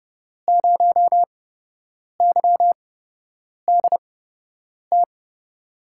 Morse code, heard as a keyed tone, about 30 wpm, spaced as for 8 wpm, 700 Hz. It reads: 0YDT
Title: USA Call Signs - 6 Characters - 8x Character Spacing - 30wpm